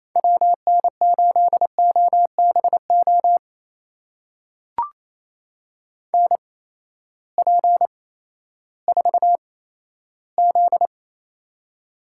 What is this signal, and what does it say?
Morse code, heard as a keyed tone, about 28 wpm, spaced as for 14 wpm, 700 Hz. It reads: WN8OBO NP4Z
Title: USA Call Signs - 8x Character Spacing - 28wpm